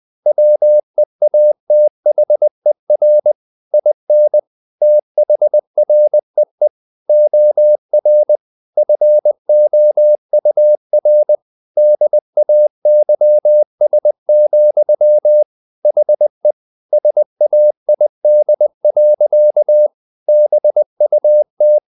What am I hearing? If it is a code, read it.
Morse code, heard as a keyed tone, about 20 wpm, 600 Hz. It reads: WEATHER IN THREE OR FOUR DAYS, HE SAID. BUT
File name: the old man and the sea part 2 20wpm